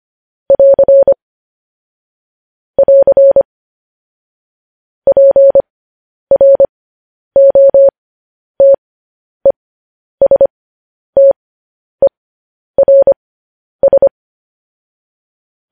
Morse code, heard as a keyed tone, about 25 wpm, spaced as for 10 wpm, 550 Hz. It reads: <AR> <AR> PROTESTERS